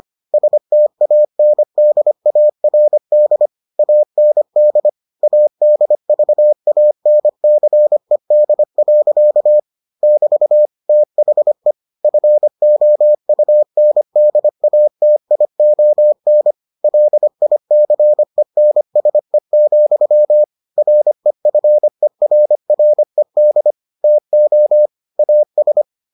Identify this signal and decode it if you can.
Morse code, heard as a keyed tone, about 25 wpm, 600 Hz. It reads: STANDARD AND ADVANCED. <BT> THE FOUNDATION LICENSE, REFERRED TO AS